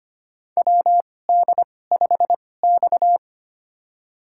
Morse code, heard as a keyed tone, about 25 wpm, spaced as for 18 wpm, 700 Hz. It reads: WD5X
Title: USA Call Signs - 4 Characters - 2x Character Spacing - 25wpm